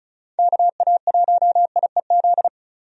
Morse code, heard as a keyed tone, about 34 wpm, 700 Hz. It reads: KA1IEZ
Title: USA Call Signs - 6 Characters - 35wpm